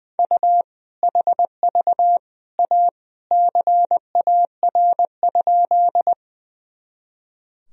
Morse code, heard as a keyed tone, about 20 wpm, 700 Hz. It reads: U HV A CAR?
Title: Sentences from Top 300 Words and Abbreviations 20wpm